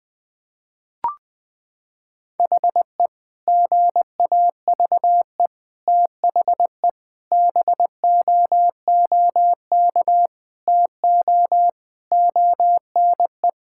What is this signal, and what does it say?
Morse code, heard as a keyed tone, about 21 wpm, 700 Hz. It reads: HE GAVE THE BOOK TO ONE
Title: Sentences with 4-Word Phrases - 20wpm